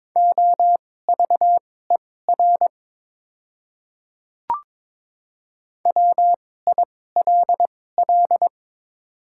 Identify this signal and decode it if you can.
Morse code, heard as a keyed tone, about 22 wpm, spaced as for 16 wpm, 690 Hz. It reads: OVER WILL
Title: 2x Character Spacing - Top 100 Words - 22wpm